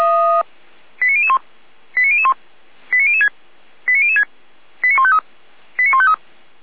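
Five-tone selective call (ZVEI), heard as quick runs of five tones, tones of about 80 ms. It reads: 89001 89001 89006 89006 89153 89153
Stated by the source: ZVEI five-tone selective call received by SDR.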